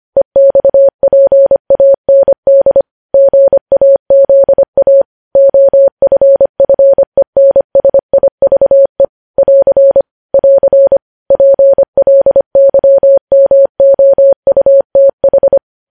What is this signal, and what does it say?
Morse code, heard as a keyed tone, about 25 wpm, 550 Hz. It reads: EXPAND GAZA OFFENSIVE <AR> <AR> PLYMOUTH
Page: news-headlines-in-morse-code-at-25-wpm